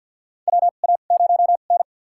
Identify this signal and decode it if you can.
Morse code, heard as a keyed tone, about 50 wpm, spaced as for 36 wpm, 700 Hz. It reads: WA0N
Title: USA Call Signs - 4 Characters - 2x Character Spacing - 50wpm